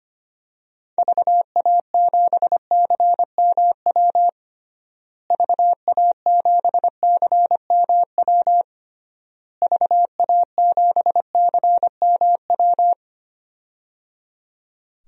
Morse code, heard as a keyed tone, about 25 wpm, 700 Hz. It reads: VA7CMW VA7CMW VA7CMW
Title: DXCC-100 Call Signs - 6 Characters - Repeated 3x - 25wpm